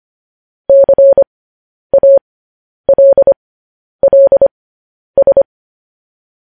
Morse code, heard as a keyed tone, about 25 wpm, spaced as for 10 wpm, 550 Hz. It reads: CALLS
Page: News Headlines in Morse Code at 10 WPM